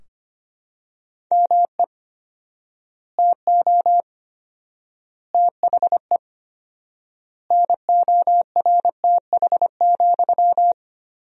Morse code, heard as a keyed tone, about 25 wpm, 700 Hz. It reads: ME TO THE NORTH,
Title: 4x Word-Spacing - Sentences from Top 400 Words - 25wpm